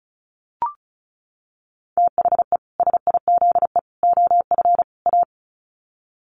Morse code, heard as a keyed tone, about 34 wpm, 700 Hz. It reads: THE SIZE OF A
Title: Sets of 4 Words 35wpm